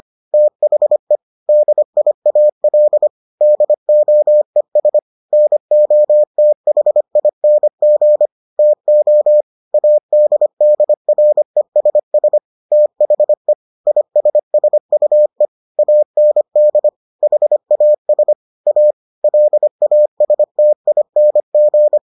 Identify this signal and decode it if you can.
Morse code, heard as a keyed tone, about 25 wpm, 600 Hz. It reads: THE DIAL DOES NOTHING TO ADDRESS THE ISSUE AND HAS A LASTING